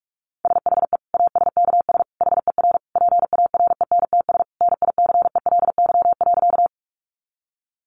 Morse code, heard as a keyed tone, about 44 wpm, 700 Hz. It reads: SHE ASKS HER PARENTS NICELY.
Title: Easy Phrases 45wpm